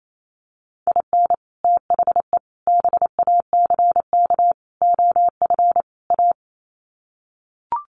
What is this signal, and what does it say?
Morse code, heard as a keyed tone, about 28 wpm, 700 Hz. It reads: IN THE BACK OF A
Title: Sets of 5 Words Encore2 28wpm